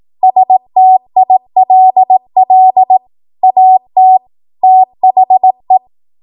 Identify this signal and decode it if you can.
Morse code, heard as a keyed tone, about 18 wpm, 750 Hz. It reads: STILL AT THE
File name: Morse / 18WPM